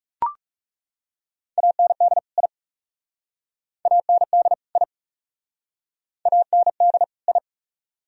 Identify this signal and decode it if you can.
Morse code, heard as a keyed tone, about 40 wpm, 700 Hz. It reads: AND I AND I AND I